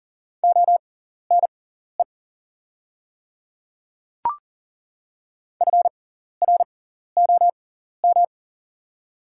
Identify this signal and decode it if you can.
Morse code, heard as a keyed tone, about 40 wpm, spaced as for 14 wpm, 690 Hz. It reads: ONE FROM
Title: 6x Character Spacing - Top 100 Words - 40wpm